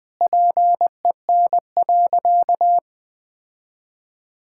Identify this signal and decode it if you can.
Morse code, heard as a keyed tone, about 21 wpm, 700 Hz. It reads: PEN.
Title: Sentences with 2-Word Phrases - 20wpm